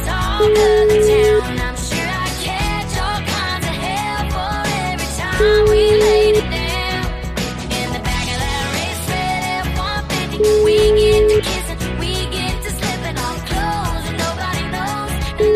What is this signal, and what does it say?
Ringback tone (Europe, 425 Hz), with noise.